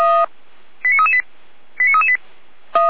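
Five-tone selective call (ZVEI), heard as quick runs of five tones, tones of about 70 ms. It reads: 89338 89338
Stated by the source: ZVEI five-tone call received on SDR.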